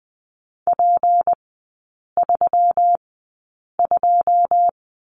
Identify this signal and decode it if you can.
Morse code, heard as a keyed tone, about 20 wpm, spaced as for 8 wpm, 700 Hz. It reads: P32
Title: 152 Lesson 25e - Letter p - Three Characters 20wpm